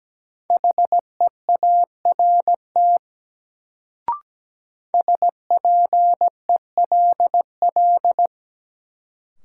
Morse code, heard as a keyed tone, about 17 wpm, 700 Hz. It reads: HEART SPELL
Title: Sentences from Top 1200 Words - New Word Drill Alt - 17wpm